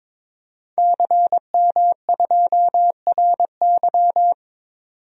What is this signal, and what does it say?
Morse code, heard as a keyed tone, about 22 wpm, 700 Hz. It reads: CM2RY
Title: DXCC-100 Call Signs - 5 Characters - 22wpm